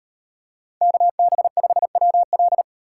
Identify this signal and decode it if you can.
Morse code, heard as a keyed tone, about 38 wpm, 700 Hz. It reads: KB5WL